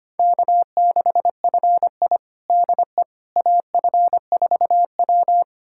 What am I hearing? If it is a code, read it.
Morse code, heard as a keyed tone, about 25 wpm, 700 Hz. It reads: K6FI DE AF4W
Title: USA Callsign DE Callsign - 4 Characters - 25wpm